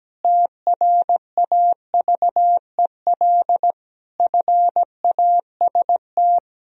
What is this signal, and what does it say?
Morse code, heard as a keyed tone, about 17 wpm, 700 Hz. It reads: TRAVEL FAST
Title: Sentences from Top 1000 Words - Review Part 3 - 17wpm